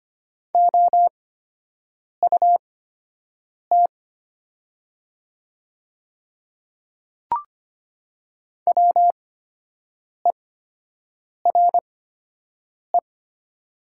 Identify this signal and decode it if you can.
Morse code, heard as a keyed tone, about 25 wpm, spaced as for 7 wpm, 690 Hz. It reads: OUT WERE
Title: 8x Character Spacing - Top 200 Words - 25wpm